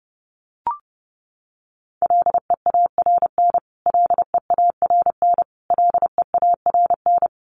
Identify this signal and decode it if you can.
Morse code, heard as a keyed tone, about 28 wpm, 700 Hz. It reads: LEARN LEARN LEARN
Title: Top 300 Words Repeated 30wpm